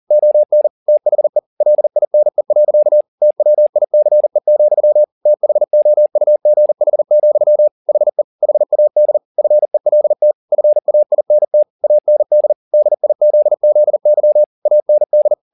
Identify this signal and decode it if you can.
Morse code, heard as a keyed tone, about 40 wpm, 600 Hz. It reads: ON THE LINE. TWICE, THOUGH, HE HAD FELT FAINT AND DIZZY AND
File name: the old man and the sea 2 40wpm